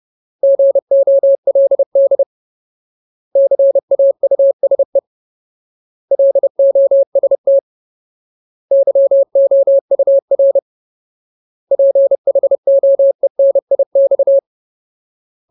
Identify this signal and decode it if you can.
Morse code, heard as a keyed tone, about 30 wpm, 550 Hz. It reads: GOLD CAUSE LOST YOUR PHOENIX
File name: Ultimate Word List / 30WPM Ultimate